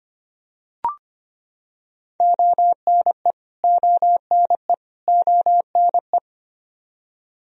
Morse code, heard as a keyed tone, about 25 wpm, 700 Hz. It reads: ONE ONE ONE
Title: Top 100 Words Repeated 25wpm